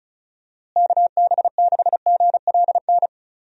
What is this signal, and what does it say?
Morse code, heard as a keyed tone, about 36 wpm, 700 Hz. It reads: KB6GLN
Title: USA Call Signs - 6 Characters - Repeated 3x - 35wpm